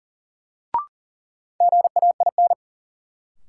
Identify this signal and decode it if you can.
Morse code, heard as a keyed tone, about 40 wpm, 700 Hz. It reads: GAIN
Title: Sentences from Top 1000 Words - New Word Drill - 40wpm